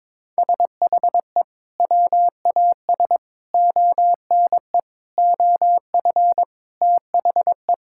Morse code, heard as a keyed tone, about 22 wpm, 700 Hz. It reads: SHE WAS ONE OF THE
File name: practice / 22wpm / Sentences with 4-Word Phrases - 22wpm